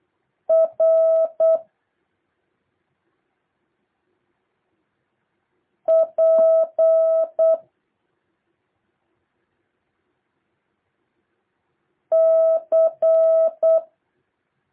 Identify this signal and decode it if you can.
Morse code, heard as a keyed tone, about 8 wpm, 650 Hz. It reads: R P C
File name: jh8cbh / 2023cw